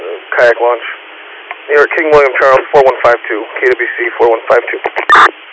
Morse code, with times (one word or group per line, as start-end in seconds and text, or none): none